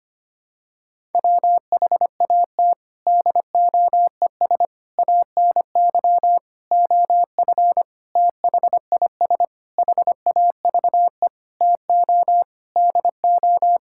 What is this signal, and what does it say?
Morse code, heard as a keyed tone, about 25 wpm, 700 Hz. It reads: WHAT DOES ANY OF THIS HAVE TO DO
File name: Sets of 9 Words - 25wpm